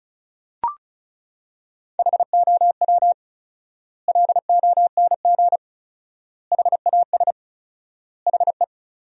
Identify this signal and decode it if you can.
Morse code, heard as a keyed tone, about 34 wpm, 700 Hz. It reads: HOW LONG HAS HE
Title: 4x Word-Spacing - Sentences from Top 100 Words - 35wpm